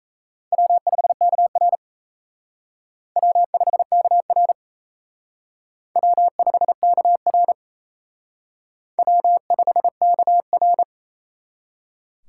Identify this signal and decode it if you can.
Morse code, heard as a keyed tone, about 36 wpm, 700 Hz. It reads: W5KR W5KR W5KR W5KR